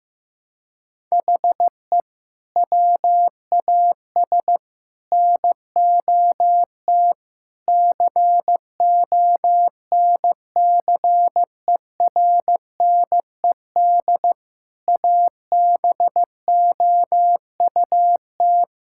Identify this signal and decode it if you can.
Morse code, heard as a keyed tone, about 16 wpm, 700 Hz. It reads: HE WAS NOT CONCERNED ABOUT
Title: Sentences from Top 1000 Words - Review Part 6 - 15wpm